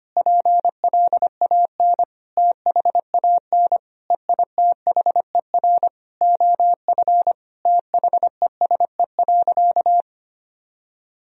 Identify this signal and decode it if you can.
Morse code, heard as a keyed tone, about 25 wpm, 700 Hz. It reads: PLAN THAN EITHER OF THESE.
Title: Sentences from Top 1000 Words - Review Part 4 - 25wpm